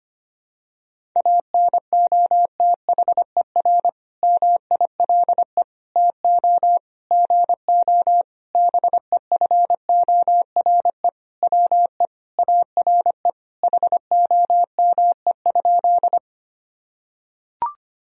Morse code, heard as a keyed tone, about 25 wpm, 700 Hz. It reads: ANOTHER MILE TO GO BEFORE WE ARE HOME?
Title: Sentences from Top 1000 Words - Review Part 1 - 25wpm